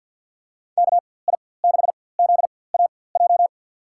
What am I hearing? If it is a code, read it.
Morse code, heard as a keyed tone, about 48 wpm, spaced as for 23 wpm, 700 Hz. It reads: KI6ZAJ
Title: USA Call Signs - 6 Characters - 4x Character Spacing - 50wpm